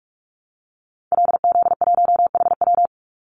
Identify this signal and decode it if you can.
Morse code, heard as a keyed tone, about 44 wpm, 700 Hz. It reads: LZ1HW
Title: Call Signs Repeated 3x - 45wpm